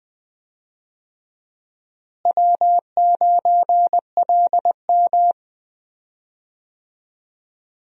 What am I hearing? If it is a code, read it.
Morse code, heard as a keyed tone, about 20 wpm, 700 Hz. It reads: W9LM
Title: USA Call Signs - 4 Characters - Repeated 2x - 20wpm